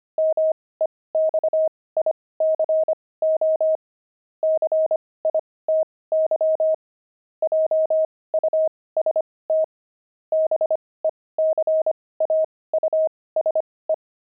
Morse code, heard as a keyed tone, about 25 wpm, spaced as for 18 wpm, 630 Hz. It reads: MEXICO CITY JUST BECAUSE
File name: pangram-fs-18-cs-25-kidding